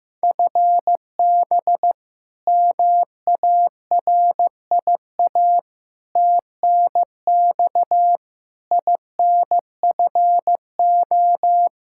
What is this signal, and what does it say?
Morse code, heard as a keyed tone, about 15 wpm, 700 Hz. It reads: FB MARIA TNX INFO
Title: K1USN SST QSO Practice 15wpm